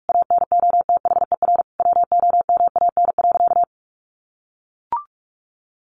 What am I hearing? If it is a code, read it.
Morse code, heard as a keyed tone, about 44 wpm, 700 Hz. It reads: ANOTHER WOMAN.
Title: Sentences from Top 1400 Words 45wpm